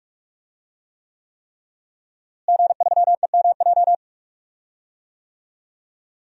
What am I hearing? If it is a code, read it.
Morse code, heard as a keyed tone, about 41 wpm, 700 Hz. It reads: G3EMJ
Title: DXCC-100 Call Signs - 5 Characters - Repeated 3x - 45wpm